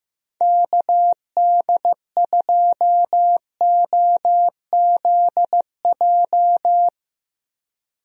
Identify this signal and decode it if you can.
Morse code, heard as a keyed tone, about 15 wpm, 700 Hz. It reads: KD2OZJ
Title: USA Call Signs - 6 Characters - Repeated 2x - 15wpm